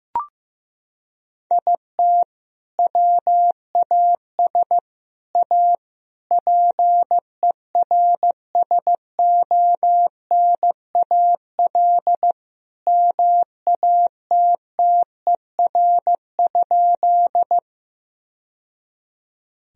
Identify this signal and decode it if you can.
Morse code, heard as a keyed tone, about 16 wpm, 700 Hz. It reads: IT WAS A PERSONAL MATTER?
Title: Sentences from Top 1000 Words - Review Part 4 - 15wpm